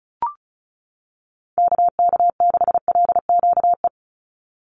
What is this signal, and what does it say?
Morse code, heard as a keyed tone, about 34 wpm, 700 Hz. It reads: KK6LQE
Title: USA Call Signs 35wpm